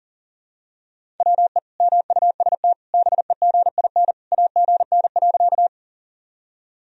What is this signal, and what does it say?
Morse code, heard as a keyed tone, about 37 wpm, 700 Hz. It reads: WE MUST BEGIN AGN.
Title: Sentences from Top 300 Words and Abbreviations 40wpm